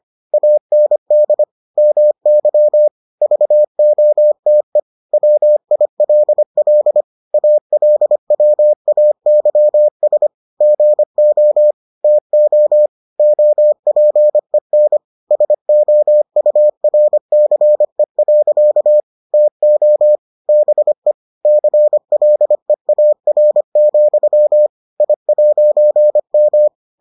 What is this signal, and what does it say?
Morse code, heard as a keyed tone, about 25 wpm, 600 Hz. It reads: AND MY VOTE WILL ALWAYS GO TO OPEN SOURCE. TO BE CLEAR, I'M